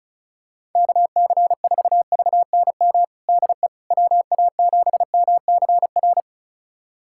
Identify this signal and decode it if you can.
Morse code, heard as a keyed tone, about 36 wpm, 700 Hz. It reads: KC4VNM DE WA7MCR
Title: USA Callsign DE Callsign - 6 Characters - 35wpm